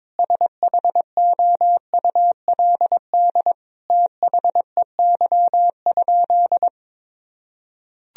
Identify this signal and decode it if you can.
Morse code, heard as a keyed tone, about 21 wpm, 700 Hz. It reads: SHOULD THEY?
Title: Sentences from Top 300 Words and Abbreviations 22wpm